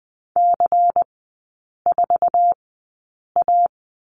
Morse code, heard as a keyed tone, about 20 wpm, spaced as for 8 wpm, 700 Hz. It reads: C4A
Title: 138 Lesson 23e - Letter f - Three Characters 20wpm